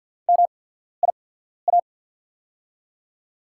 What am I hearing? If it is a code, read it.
Morse code, heard as a keyed tone, about 48 wpm, spaced as for 14 wpm, 700 Hz. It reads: MIA
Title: USA Call Signs - 6 Characters - 8x Character Spacing - 50wpm